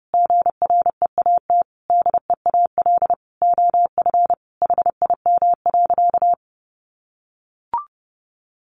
Morse code, heard as a keyed tone, about 28 wpm, 700 Hz. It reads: GREAT DEAL OF HIM.
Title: Sentences from Top 800 Words 30wpm